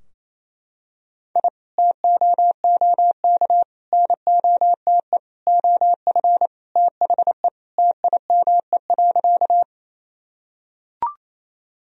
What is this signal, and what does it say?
Morse code, heard as a keyed tone, about 27 wpm, 700 Hz. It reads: I TOOK NOTE OF THE TIME.
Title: Sentences from Top 1000 Words - Review Part 2 - 28wpm